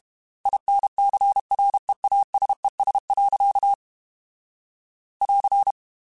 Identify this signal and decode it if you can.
Morse code, heard as a keyed tone, about 32 wpm, 800 Hz. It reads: INCREASES. <AR>
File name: Technical Subjects / Technical Subjects-mp3s / XL @30